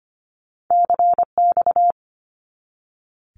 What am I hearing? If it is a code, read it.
Morse code, heard as a keyed tone, about 25 wpm, 700 Hz. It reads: CX
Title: Abbreviations 25wpm